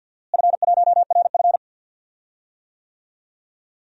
Morse code, heard as a keyed tone, about 51 wpm, 700 Hz. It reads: F1RF
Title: DXCC-100 Call Signs - 4 Characters - Repeated 2x - 50wpm